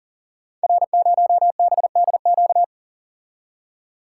Morse code, heard as a keyed tone, about 40 wpm, 690 Hz. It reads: R0BDQ
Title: Code Groups - 5 Letter:Numbers - 40wpm